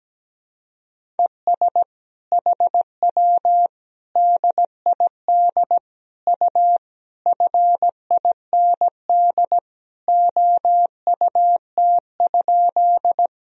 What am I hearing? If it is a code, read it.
Morse code, heard as a keyed tone, about 17 wpm, 700 Hz. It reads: ES HW DID U FIND OUT?
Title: Sentences from Top 100 Words and Abbreviations 17wpm